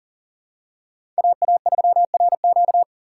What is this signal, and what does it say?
Morse code, heard as a keyed tone, about 40 wpm, 700 Hz. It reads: AA3RQ
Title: USA Call Signs - 5 Characters - 2x Character Spacing - 40wpm